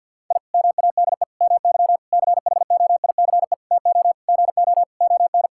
Morse code, heard as a keyed tone, about 46 wpm, 700 Hz. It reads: I MADE MY CHOICE TO GO ON
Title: Sentences from Top 1000 Words - Review Part 6 - 50wpm